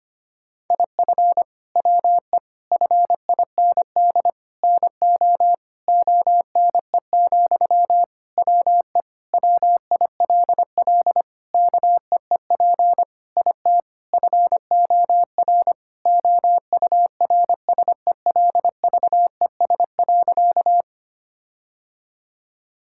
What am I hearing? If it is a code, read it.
Morse code, heard as a keyed tone, about 25 wpm, 700 Hz. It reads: IF WE FIND NO ONE, WE WILL KEEP IT FOR OURSELVES.
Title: Sentences from Top 1600 Words 25wpm